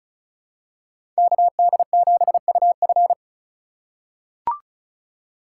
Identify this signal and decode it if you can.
Morse code, heard as a keyed tone, about 36 wpm, 700 Hz. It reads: KD7UF
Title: USA Call Signs - 5 Characters - 4x Character Spacing - 35wpm